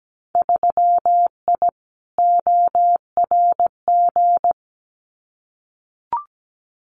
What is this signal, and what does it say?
Morse code, heard as a keyed tone, about 17 wpm, 700 Hz. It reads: 3I ORG
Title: Winter Field Day Exchanges Encore 17wpm